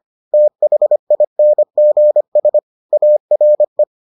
Morse code, heard as a keyed tone, about 25 wpm, 600 Hz. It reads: THINGS ARE